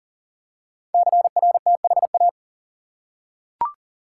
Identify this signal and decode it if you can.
Morse code, heard as a keyed tone, about 41 wpm, 690 Hz. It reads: CRTHA
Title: Code Groups - 5 Letters - 40wpm